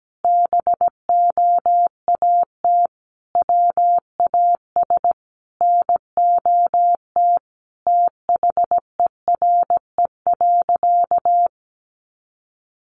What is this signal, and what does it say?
Morse code, heard as a keyed tone, about 17 wpm, 700 Hz. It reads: BOAT WAS NOT THERE.
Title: Sentences from Top 1400 Words 17wpm